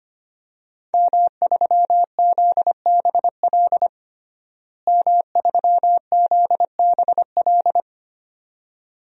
Morse code, heard as a keyed tone, about 25 wpm, 700 Hz. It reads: M3ZBL M3ZBL
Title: DXCC-100 Call Signs - 5 Characters - Repeated 2x - 25wpm